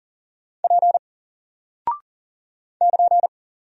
Morse code, heard as a keyed tone, about 40 wpm, 700 Hz. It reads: P <KN>
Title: Single Character (Letter:Number:Punctuation:Prosign) Alt 40wpm